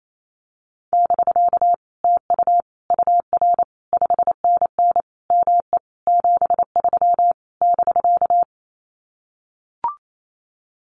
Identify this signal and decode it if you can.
Morse code, heard as a keyed tone, about 28 wpm, 700 Hz. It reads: <BK> TU UR 5NN ME 73 <BK>
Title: Common QSO Phrases Encore 28wpm